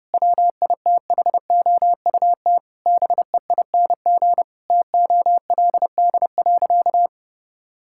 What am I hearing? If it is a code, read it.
Morse code, heard as a keyed tone, about 32 wpm, 700 Hz. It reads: WITHOUT BEING TOLD.
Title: Sentences from Top 1000 Words - Review Part 6 - 30wpm